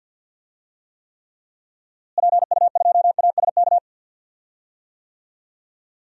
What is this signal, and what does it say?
Morse code, heard as a keyed tone, about 51 wpm, 700 Hz. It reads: PU2ASK
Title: DXCC-100 Call Signs - 6 Characters - Repeated 3x - 50wpm